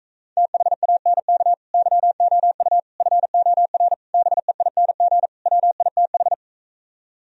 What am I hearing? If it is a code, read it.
Morse code, heard as a keyed tone, about 45 wpm, 700 Hz. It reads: THANK YOU FOR BEING WITH